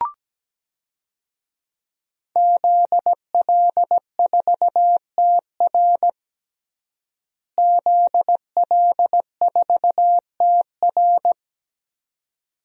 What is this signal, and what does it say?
Morse code, heard as a keyed tone, about 17 wpm, 700 Hz. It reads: ZL4TR ZL4TR
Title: DXCC-100 Call Signs - 5 Characters - Repeated 2x - 17wpm